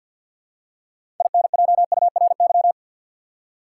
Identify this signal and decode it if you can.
Morse code, heard as a keyed tone, about 53 wpm, 700 Hz. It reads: INJURY
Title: Six Letter Words - 50wpm